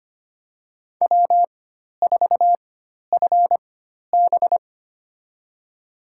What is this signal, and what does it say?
Morse code, heard as a keyed tone, about 25 wpm, spaced as for 12 wpm, 700 Hz. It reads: W4FB
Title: USA Call Signs - 4x Character Spacing - 25wpm